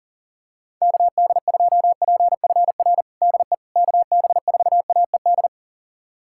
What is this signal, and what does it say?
Morse code, heard as a keyed tone, about 37 wpm, 700 Hz. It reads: KD2PFR DE KB4AED
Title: USA Callsign DE Callsign - 6 Characters - 40wpm